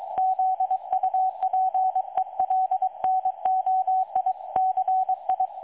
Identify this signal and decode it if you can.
Morse code, heard as a keyed tone, about 23 wpm, 750 Hz, with radio noise. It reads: ZUPELNOICI
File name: dlazaawansowanych / nagraniazpasma / websdr recording 3565.0kHz